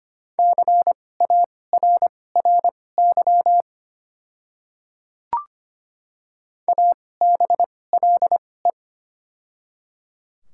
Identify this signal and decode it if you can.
Morse code, heard as a keyed tone, about 25 wpm, spaced as for 18 wpm, 700 Hz. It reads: CARRY ABLE
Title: 2x Character Spacing - Top 500 Words - 25wpm